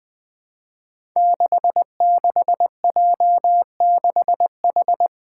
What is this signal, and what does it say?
Morse code, heard as a keyed tone, about 20 wpm, 700 Hz. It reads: 66J6H
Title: Five Challenging Letter-Number Combinations 20wpm